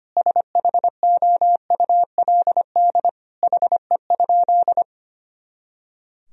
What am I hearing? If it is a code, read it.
Morse code, heard as a keyed tone, about 25 wpm, 700 Hz. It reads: SHOULD HE?
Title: Sentences from Top 300 Words and Abbreviations 25wpm